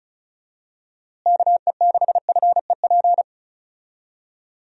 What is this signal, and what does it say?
Morse code, heard as a keyed tone, about 36 wpm, 700 Hz. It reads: KE6FEP
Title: USA Call Signs - 6 Characters - Repeated 2x - 35wpm